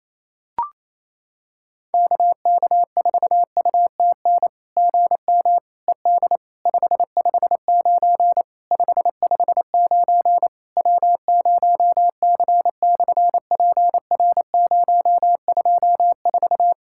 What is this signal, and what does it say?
Morse code, heard as a keyed tone, about 28 wpm, 690 Hz. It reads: KK4UTN GM ED 559 559 W0C/PR024
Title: SOTA QSO Practice 28wpm